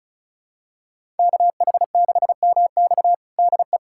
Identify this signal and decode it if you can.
Morse code, heard as a keyed tone, about 36 wpm, 700 Hz. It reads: KH6MX DE
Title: USA Callsign DE Callsign - 5 Characters - 35wpm